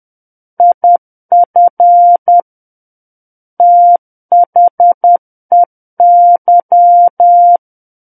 Morse code, heard as a keyed tone, about 10 wpm, 700 Hz. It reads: IF THEY
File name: Book One / 01-Chapter 1 - 10 WPM